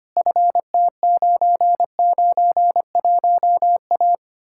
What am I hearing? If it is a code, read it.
Morse code, heard as a keyed tone, about 33 wpm, 700 Hz. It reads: FT991A